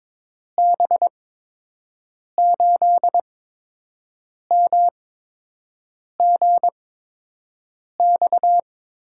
Morse code, heard as a keyed tone, about 22 wpm, spaced as for 6 wpm, 700 Hz. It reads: B8MGX
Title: USA Call Signs - 6 Characters - 8x Character Spacing - 22wpm